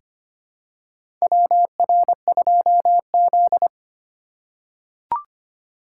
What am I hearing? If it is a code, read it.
Morse code, heard as a keyed tone, about 25 wpm, 700 Hz. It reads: WR2Z
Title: USA Call Signs - 2x Character Spacing - 25wpm